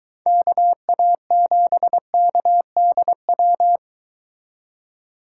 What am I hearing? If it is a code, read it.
Morse code, heard as a keyed tone, about 23 wpm, 700 Hz. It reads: KA7KDW